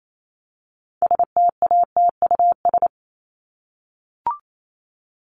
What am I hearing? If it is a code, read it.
Morse code, heard as a keyed tone, about 28 wpm, 700 Hz. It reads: STATUS
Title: Top 1251-1500 Words 28wpm